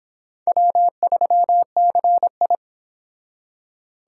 Morse code, heard as a keyed tone, about 25 wpm, 700 Hz. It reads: W3CI